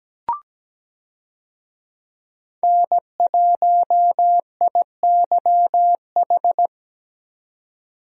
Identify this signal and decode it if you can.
Morse code, heard as a keyed tone, about 17 wpm, 700 Hz. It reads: N1IYH
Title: USA Call Signs - 5 Characters - 17wpm